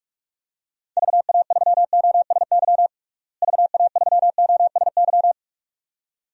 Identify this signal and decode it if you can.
Morse code, heard as a keyed tone, about 41 wpm, 700 Hz. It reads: VA3OSY VA3OSY
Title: DXCC-100 Call Signs - 6 Characters - Repeated 2x - 45wpm